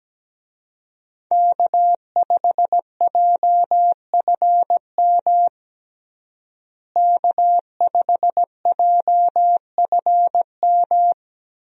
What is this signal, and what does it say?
Morse code, heard as a keyed tone, about 17 wpm, 700 Hz. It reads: K5JFM K5JFM
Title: USA Call Signs - 5 Characters - Repeated 3x - 17wpm